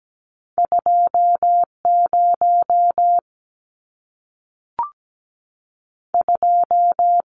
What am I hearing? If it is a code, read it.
Morse code, heard as a keyed tone, about 17 wpm, 700 Hz. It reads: 20 2
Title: Numbers - 2-Digit 17wpm